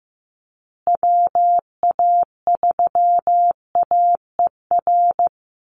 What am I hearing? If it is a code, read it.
Morse code, heard as a keyed tone, about 15 wpm, 700 Hz. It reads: WA3AER
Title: Call Signs Repeated 2x - 15wpm